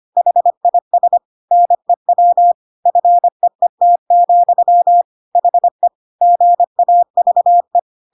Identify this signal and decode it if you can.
Morse code, heard as a keyed tone, about 25 wpm, 700 Hz. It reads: HIS NEW FEET, HE GAVE
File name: Pinocchio / Pinocchio ch8 25 25 ik1hge